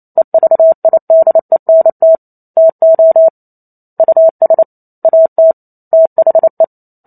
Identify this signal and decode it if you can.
Morse code, heard as a keyed tone, about 28 wpm, 650 Hz. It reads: EVIDENT TO US AT THE